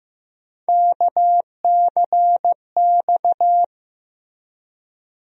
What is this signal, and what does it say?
Morse code, heard as a keyed tone, about 15 wpm, 700 Hz. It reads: KCX
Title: USA Call Sign Postfixes - 3 Letters 15wpm